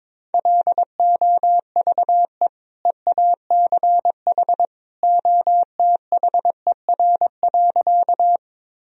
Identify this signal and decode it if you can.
Morse code, heard as a keyed tone, about 22 wpm, 700 Hz. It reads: LOVE EACH OTHER.
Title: Sentences from Top 1000 Words - Review Part 1 - 22wpm